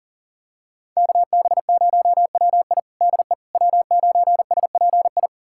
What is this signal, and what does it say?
Morse code, heard as a keyed tone, about 40 wpm, 690 Hz. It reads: KB0WI DE W9SPI
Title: USA Callsign DE Callsign - 5 Characters - 40wpm